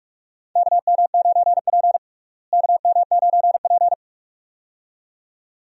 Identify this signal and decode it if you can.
Morse code, heard as a keyed tone, about 45 wpm, 700 Hz. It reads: KM9P KM9P
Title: USA Call Signs - 4 Characters - Repeated 2x - 45wpm